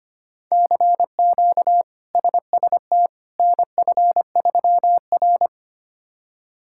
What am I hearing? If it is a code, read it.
Morse code, heard as a keyed tone, about 25 wpm, 700 Hz. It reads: CQ SST NF3R
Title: K1USN SST QSO Practice - 25wpm